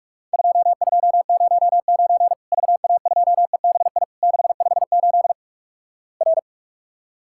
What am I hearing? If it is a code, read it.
Morse code, heard as a keyed tone, about 46 wpm, 700 Hz. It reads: 2209 VA2EBI 658 R
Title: Worked All Europe DX Contest - QTC Exchange - 45wpm